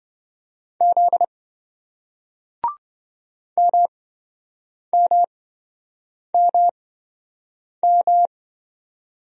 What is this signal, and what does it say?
Morse code, heard as a keyed tone, about 26 wpm, spaced as for 7 wpm, 700 Hz. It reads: Z MMMM